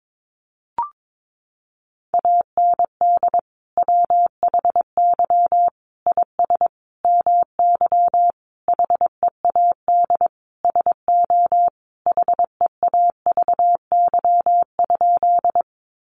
Morse code, heard as a keyed tone, about 22 wpm, 700 Hz. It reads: AND WHY IS MY HEAD SO HEAVY?
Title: Sentences from Top 1100 Words 22wpm